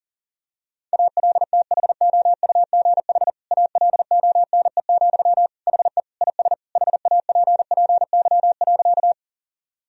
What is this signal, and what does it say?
Morse code, heard as a keyed tone, about 40 wpm, 700 Hz. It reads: ALTHOUGH ALONE, HE IS HAPPY.
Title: Sentences from Top 1000 Words - Review Part 3 - 40wpm